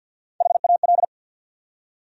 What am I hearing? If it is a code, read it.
Morse code, heard as a keyed tone, about 51 wpm, 700 Hz. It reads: HAL
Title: USA Common Names - Max 3 Letters - 50wpm